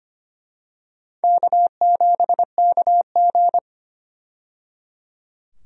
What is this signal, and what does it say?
Morse code, heard as a keyed tone, about 25 wpm, 700 Hz. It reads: K7KG
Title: USA Call Signs - 4 Characters - 25wpm